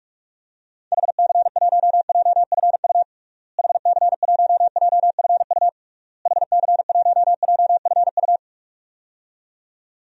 Morse code, heard as a keyed tone, about 46 wpm, 700 Hz. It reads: HC1JFU HC1JFU HC1JFU
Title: DXCC-100 Call Signs - 6 Characters - Repeated 3x - 45wpm